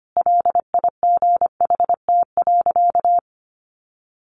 Morse code, heard as a keyed tone, about 25 wpm, 700 Hz. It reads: LIGHT.
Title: A Sound Of Thunder 25wpm